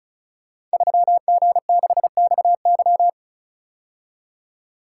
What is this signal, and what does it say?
Morse code, heard as a keyed tone, about 36 wpm, 700 Hz. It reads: 3G6XY
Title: Code Groups - 5 Letter:Numbers - 35wpm